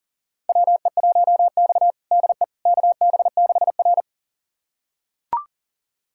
Morse code, heard as a keyed tone, about 40 wpm, 700 Hz. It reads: WE1X DE KB6R
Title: USA Callsign DE Callsign - 4 Characters - 40wpm